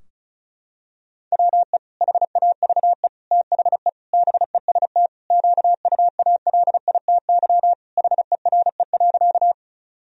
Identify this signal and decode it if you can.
Morse code, heard as a keyed tone, about 36 wpm, 700 Hz. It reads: WE HAVE THE BEST QUALITY HERE.
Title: Sentences from Top 1000 Words - Review Part 4 - 35wpm